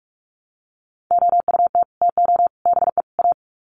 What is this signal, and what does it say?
Morse code, heard as a keyed tone, about 44 wpm, 700 Hz. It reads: OUT TO BE A
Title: Sets of 4 Words 45wpm